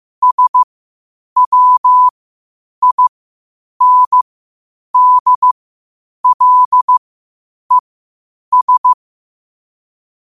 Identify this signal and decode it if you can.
Morse code, heard as a keyed tone, about 15 wpm, spaced as for 8 wpm, 1000 Hz. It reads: SWINDLES